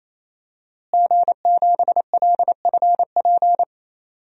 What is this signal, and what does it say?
Morse code, heard as a keyed tone, about 28 wpm, 700 Hz. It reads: G7LFP
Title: DXCC-100 Call Signs - 5 Characters - Repeated 3x - 28wpm